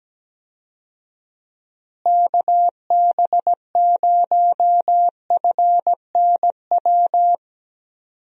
Morse code, heard as a keyed tone, about 17 wpm, 700 Hz. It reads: KB0FNW
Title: USA Call Signs - 6 Characters - Repeated 2x - 17wpm